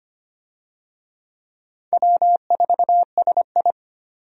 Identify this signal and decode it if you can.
Morse code, heard as a keyed tone, about 25 wpm, 700 Hz. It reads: W4SI
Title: USA Call Signs - 4 Characters - 25wpm